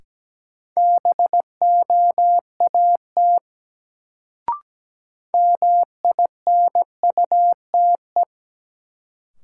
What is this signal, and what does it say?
Morse code, heard as a keyed tone, about 17 wpm, 700 Hz. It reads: BOAT MINUTE
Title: Sentences from Top 500 Words - New Word Drill - 17wpm